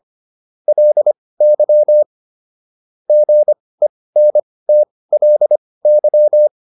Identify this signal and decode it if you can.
Morse code, heard as a keyed tone, about 25 wpm, spaced as for 18 wpm, 600 Hz. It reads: LY GENTLY
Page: intermediate-practice-files